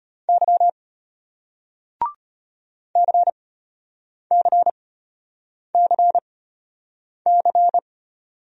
Morse code, heard as a keyed tone, about 33 wpm, spaced as for 8 wpm, 700 Hz. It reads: Y CCCC